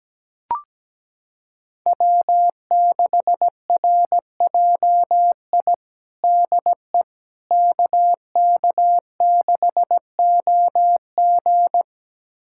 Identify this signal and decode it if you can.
Morse code, heard as a keyed tone, about 17 wpm, 700 Hz. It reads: W6RJI DE KK6OG
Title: USA Callsign DE Callsign - 5 Characters - 17wpm